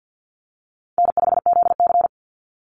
Morse code, heard as a keyed tone, about 49 wpm, 700 Hz. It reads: N5ZC
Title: Call Signs - 50wpm